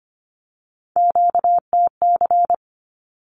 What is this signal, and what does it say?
Morse code, heard as a keyed tone, about 25 wpm, 700 Hz. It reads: QTC
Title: Q-Codes 25wpm